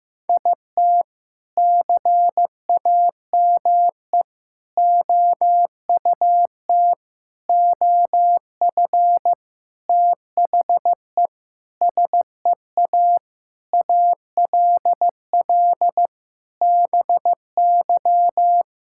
Morse code, heard as a keyed tone, about 15 wpm, 700 Hz. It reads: IT CAME OUT OF THE SEA ALL BY
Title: Sentences from Top 1000 Words - Review Part 5 - 15wpm